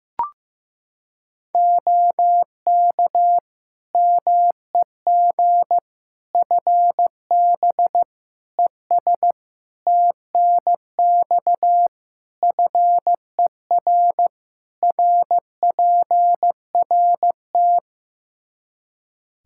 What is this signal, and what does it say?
Morse code, heard as a keyed tone, about 15 wpm, 700 Hz. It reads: OK MEG FB ES TNX FER RPRT
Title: Practice QSO - Exchange 2 - Info - 15wpm